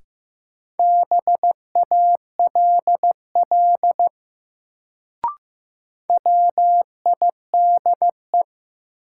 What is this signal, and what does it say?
Morse code, heard as a keyed tone, about 15 wpm, 700 Hz. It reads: BALL WIDE
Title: Sentences from Top 900 Words - New Word Drill - 15wpm